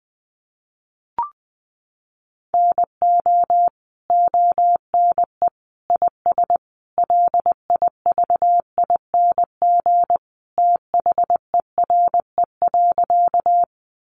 Morse code, heard as a keyed tone, about 20 wpm, 700 Hz. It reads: NO ONE IS LIVING THERE.
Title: Sentences from Top 1100 Words 20wpm